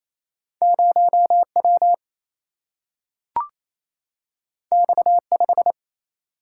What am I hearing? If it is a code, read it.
Morse code, heard as a keyed tone, about 28 wpm, 700 Hz. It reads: 0 W X 5
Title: Single-Letter Single-Number 28wpm